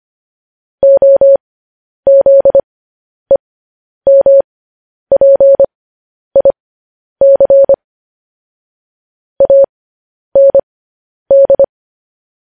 Morse code, heard as a keyed tone, about 25 wpm, spaced as for 10 wpm, 550 Hz. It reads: OZEMPIC AND
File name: podcast / News headlines-1x-25x10